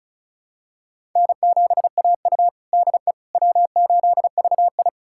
Morse code, heard as a keyed tone, about 36 wpm, 700 Hz. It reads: N7AU DE W8VI
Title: USA Callsign DE Callsign - 4 Characters - 35wpm